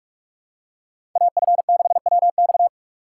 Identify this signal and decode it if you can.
Morse code, heard as a keyed tone, about 44 wpm, 700 Hz. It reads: AF6WX